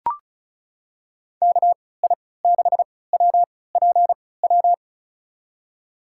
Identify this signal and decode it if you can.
Morse code, heard as a keyed tone, about 36 wpm, spaced as for 20 wpm, 700 Hz. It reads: KI6WPW
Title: USA Call Signs - 3x Character Spacing - 35wpm